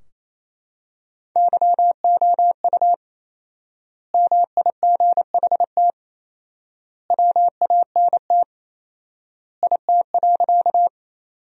Morse code, heard as a keyed tone, about 28 wpm, 700 Hz. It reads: YOU MIGHT WANT IT.
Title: 4x Word-Spacing - Sentences from Top 300 Words - 28wpm